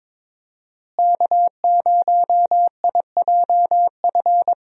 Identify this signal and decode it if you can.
Morse code, heard as a keyed tone, about 23 wpm, 700 Hz. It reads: K0IJF
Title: USA Call Signs - 5 Characters - Repeated 3x - 22wpm